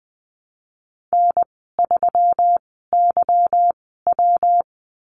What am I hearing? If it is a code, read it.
Morse code, heard as a keyed tone, about 20 wpm, spaced as for 15 wpm, 700 Hz. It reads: N3YW
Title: 154 Lesson 25g - Letter p - Call Signs 20wpm